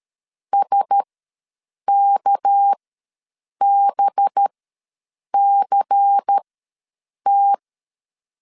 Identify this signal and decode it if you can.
Morse code, heard as a keyed tone, about 13 wpm, spaced as for 7 wpm, 800 Hz. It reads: SKBCT
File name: LEZIONI / 6 PKX0 / 6 PKX ws 12wpm 3106